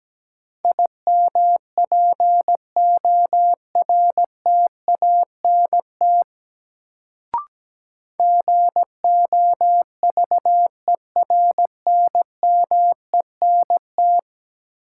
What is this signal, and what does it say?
Morse code, heard as a keyed tone, about 17 wpm, 700 Hz. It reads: IMPORTANT GOVERNMENT
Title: Sentences from Top 600 Words - New Word Drill - 17wpm